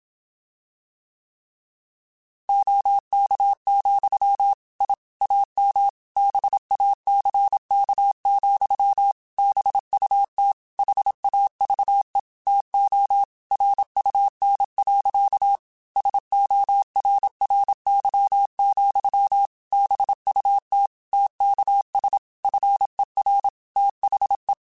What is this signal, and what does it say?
Morse code, heard as a keyed tone, about 27 wpm, 800 Hz. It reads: OK, I AM BACK, BUT HAVE TO RUN. SORRY, BUT TKS FER THE
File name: QSOs / QSOs-mp3s / kc7sac@25